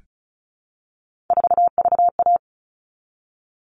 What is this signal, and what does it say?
Morse code, heard as a keyed tone, about 34 wpm, 700 Hz. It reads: 4VA
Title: All Cut Numbers - 3-Digit 35wpm